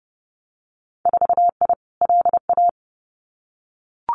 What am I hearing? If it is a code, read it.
Morse code, heard as a keyed tone, about 30 wpm, 700 Hz. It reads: 4I LA
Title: Winter Field Day Exchanges Encore 30wpm